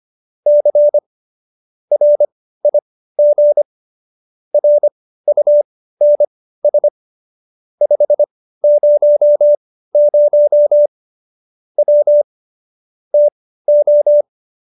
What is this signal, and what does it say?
Morse code, heard as a keyed tone, about 25 wpm, spaced as for 15 wpm, 590 Hz. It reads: C RIG RUNS 500 W TO